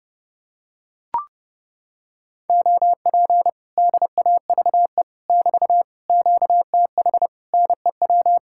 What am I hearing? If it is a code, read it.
Morse code, heard as a keyed tone, about 29 wpm, 700 Hz. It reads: OP DAVE <BT> QTH NEW